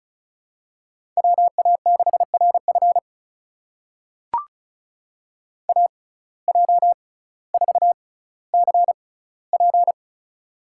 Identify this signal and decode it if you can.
Morse code, heard as a keyed tone, about 36 wpm, spaced as for 18 wpm, 700 Hz. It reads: WA6RF AJ4CP
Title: USA Call Signs - 5 Characters - 6x Character Spacing - 35wpm